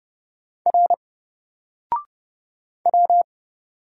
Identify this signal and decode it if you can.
Morse code, heard as a keyed tone, about 30 wpm, 700 Hz. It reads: R W
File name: Single-Letter-Number-30wpm